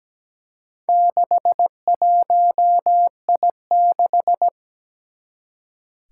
Morse code, heard as a keyed tone, about 17 wpm, 700 Hz. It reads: 61I6
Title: Four Challenging Letter-Number Combinations 17wpm